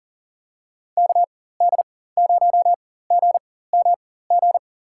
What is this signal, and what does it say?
Morse code, heard as a keyed tone, about 40 wpm, spaced as for 19 wpm, 700 Hz. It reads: KD0GMG
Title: USA Call Signs - 6 Characters - 4x Character Spacing - 40wpm